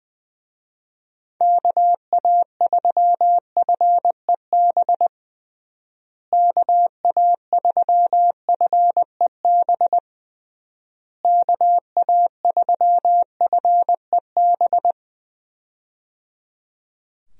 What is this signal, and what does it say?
Morse code, heard as a keyed tone, about 20 wpm, 700 Hz. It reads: KA3FEB KA3FEB KA3FEB
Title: USA Call Signs - 6 Characters - Repeated 3x - 20wpm